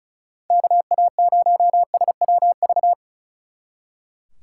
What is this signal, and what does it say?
Morse code, heard as a keyed tone, about 44 wpm, 700 Hz. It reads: KA0SWV